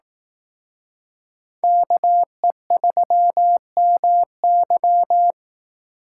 Morse code, heard as a keyed tone, about 18 wpm, 700 Hz. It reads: KE3MY
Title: Speed Racing - USA 2x2 Call Signs 15wpm